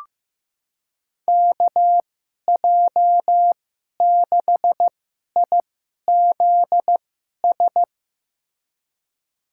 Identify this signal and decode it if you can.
Morse code, heard as a keyed tone, about 16 wpm, spaced as for 11 wpm, 700 Hz. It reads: KJ6IZS
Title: USA Call Signs - 6 Characters - 2x Character Spacing - 15wpm